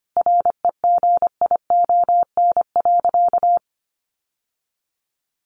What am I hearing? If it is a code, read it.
Morse code, heard as a keyed tone, about 25 wpm, 700 Hz. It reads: REGION.
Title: Sentences from Top 1500 Words 25wpm